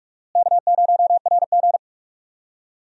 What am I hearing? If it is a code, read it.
Morse code, heard as a keyed tone, about 42 wpm, 700 Hz. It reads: K0RG
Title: USA Call Signs - 4 Characters - Repeated 2x - 45wpm